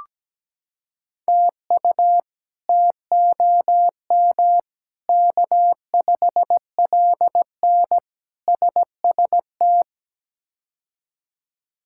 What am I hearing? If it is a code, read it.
Morse code, heard as a keyed tone, about 17 wpm, 700 Hz. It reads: TU TOM K5LN SST
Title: K1USN SST QSO Practice - 17wpm